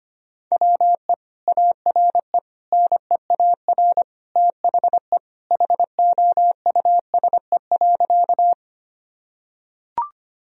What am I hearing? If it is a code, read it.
Morse code, heard as a keyed tone, about 25 wpm, 700 Hz. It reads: WE ARE NEAR THE HOUSE.
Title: Sentences from Top 1000 Words - Review Part 1 - 25wpm